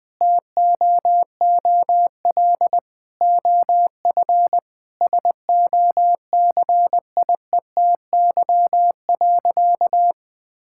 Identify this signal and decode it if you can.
Morse code, heard as a keyed tone, about 21 wpm, 700 Hz. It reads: TOOL OF SOCIETY.
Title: Sentences from Top 1000 Words - Review Part 6 - 20wpm